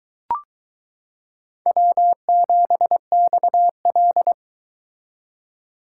Morse code, heard as a keyed tone, about 22 wpm, 700 Hz. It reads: W7XL